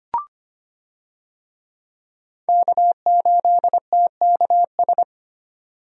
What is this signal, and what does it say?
Morse code, heard as a keyed tone, about 25 wpm, 690 Hz. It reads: K8TKS
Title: USA Call Signs - 5 Characters - 25wpm